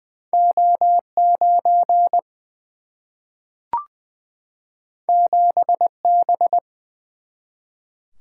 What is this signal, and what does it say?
Morse code, heard as a keyed tone, about 20 wpm, 700 Hz. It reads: O9 7B
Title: Single-Letter Single-Number 20wpm